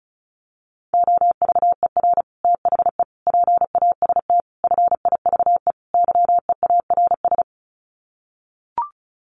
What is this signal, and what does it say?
Morse code, heard as a keyed tone, about 34 wpm, 700 Hz. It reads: OVER THE PAST FIVE YEARS
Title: Sets of 5 Words Encore 35wpm